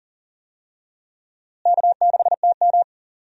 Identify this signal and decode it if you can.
Morse code, heard as a keyed tone, about 40 wpm, 700 Hz. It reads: K6TM
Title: USA Call Signs - 4 Characters - Repeated 2x - 40wpm